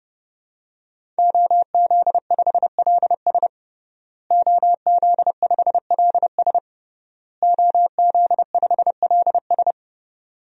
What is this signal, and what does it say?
Morse code, heard as a keyed tone, about 30 wpm, 700 Hz. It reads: OZ5LS OZ5LS OZ5LS
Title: DXCC-100 Call Signs - 5 Characters - Repeated 3x - 30wpm